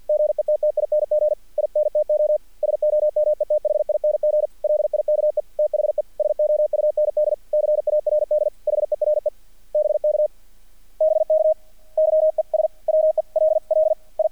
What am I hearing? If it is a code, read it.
Morse code, heard as a keyed tone, about 48 wpm, 610 Hz. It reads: GETTING INTO SOMETHING LIKE THE SOUND CARD HERE BK BK YES WELL I
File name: archived / CRSnet